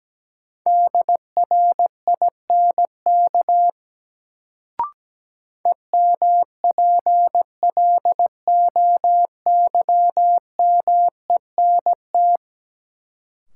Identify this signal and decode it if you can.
Morse code, heard as a keyed tone, about 17 wpm, 700 Hz. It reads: DRINK EMPLOYMENT
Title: Sentences from Top 1500 Words - New Word Drill - 17wpm